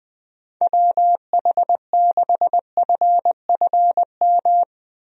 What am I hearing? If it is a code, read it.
Morse code, heard as a keyed tone, about 20 wpm, 700 Hz. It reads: WH6FFM
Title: USA Call Signs - 6 Characters - 20wpm